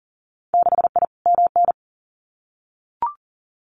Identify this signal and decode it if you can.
Morse code, heard as a keyed tone, about 40 wpm, 700 Hz. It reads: 6I MN
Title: Winter Field Day Exchanges Encore 40wpm